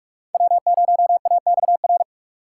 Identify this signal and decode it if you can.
Morse code, heard as a keyed tone, about 46 wpm, 700 Hz. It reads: W0AXR
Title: USA Call Signs - 5 Characters - 3x Character Spacing - 45wpm